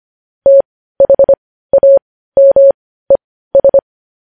Morse code, heard as a keyed tone, about 25 wpm, spaced as for 15 wpm, 550 Hz. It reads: THAMES